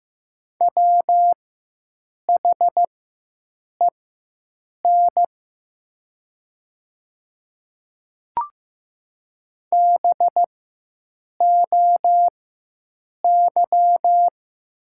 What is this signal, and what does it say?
Morse code, heard as a keyed tone, about 15 wpm, spaced as for 7 wpm, 700 Hz. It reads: WHEN BOY
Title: 4x Character Spacing - Top 300 Words - 15wpm